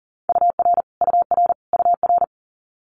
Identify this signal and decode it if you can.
Morse code, heard as a keyed tone, about 42 wpm, 700 Hz. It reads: UR UR UR
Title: Multi-letter Phonemes Repeated 3x 40wpm